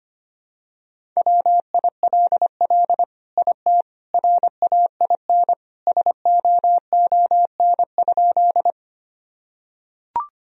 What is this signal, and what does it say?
Morse code, heard as a keyed tone, about 25 wpm, 700 Hz. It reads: WILL IT RAIN SOON?
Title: Sentences from Top 1000 Words - Review Part 1 - 25wpm